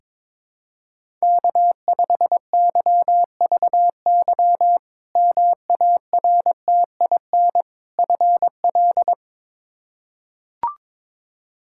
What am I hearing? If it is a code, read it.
Morse code, heard as a keyed tone, about 22 wpm, 700 Hz. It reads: K5YVY MARTIN FL
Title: K1USN SST QSO Practice - 22wpm